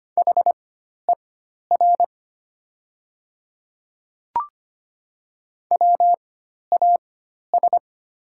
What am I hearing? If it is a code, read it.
Morse code, heard as a keyed tone, about 25 wpm, spaced as for 12 wpm, 700 Hz. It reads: HER WAS
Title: 4x Character Spacing - Top 100 Words - 25wpm